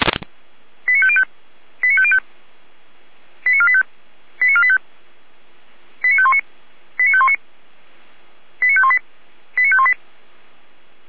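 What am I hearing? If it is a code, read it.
Five-tone selective call (ZVEI), heard as quick runs of five tones, tones of about 70 ms. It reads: 89505 89505 89485 89485 89519 89519 89518 89518
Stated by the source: ZVEI five-tone selective call received by SDR.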